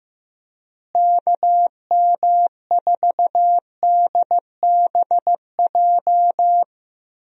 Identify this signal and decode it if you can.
Morse code, heard as a keyed tone, about 15 wpm, 700 Hz. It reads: KM4DBJ
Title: USA Call Signs - 6 Characters - Repeated 2x - 15wpm